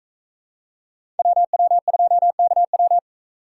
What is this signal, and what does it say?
Morse code, heard as a keyed tone, about 41 wpm, 700 Hz. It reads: WW2KW